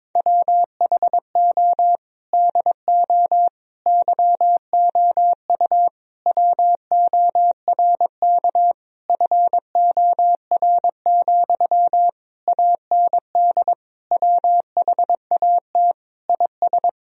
Morse code, heard as a keyed tone, about 22 wpm, 700 Hz. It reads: WHO DO YOU WORK FOR, AND WHAT IS
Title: Sentences from Top 1000 Words - Review Part 4 - 22wpm